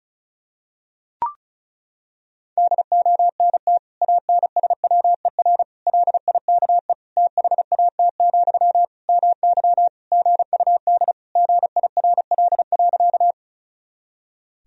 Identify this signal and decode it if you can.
Morse code, heard as a keyed tone, about 36 wpm, 700 Hz. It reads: DONT ANSWER LIKE THAT, MY GUD GIRL.
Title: Sentences from Top 400 Words and Abbreviations 35wpm